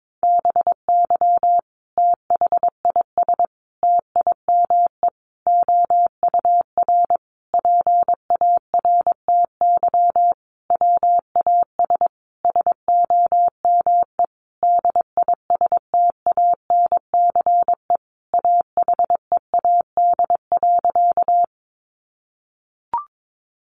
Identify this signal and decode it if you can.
Morse code, heard as a keyed tone, about 22 wpm, 700 Hz. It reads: BY THIS TIME OUR PARTY WAS SOME DISTANCE AHEAD.
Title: Sentences from Top 1400 Words 22wpm